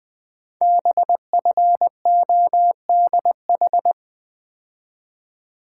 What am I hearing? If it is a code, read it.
Morse code, heard as a keyed tone, about 20 wpm, 700 Hz. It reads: BFODH
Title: Code Groups - 5 Letters - 20wpm